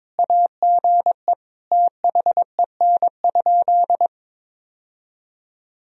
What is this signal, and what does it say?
Morse code, heard as a keyed tone, about 22 wpm, 700 Hz. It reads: AGE THEN?
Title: Sentences from Top 1000 Words - Review Part 3 - 22wpm